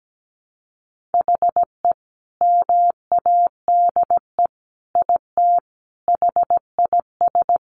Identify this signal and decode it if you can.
Morse code, heard as a keyed tone, about 17 wpm, 700 Hz. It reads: HE MADE IT HIS
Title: Sentences from Top 1400 Words 17wpm